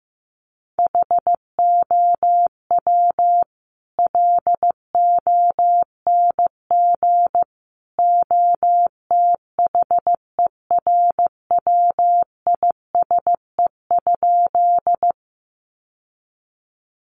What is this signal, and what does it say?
Morse code, heard as a keyed tone, about 15 wpm, 700 Hz. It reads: HOW LONG OTHERWISE?
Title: Sentences from Top 1200 Words 15wpm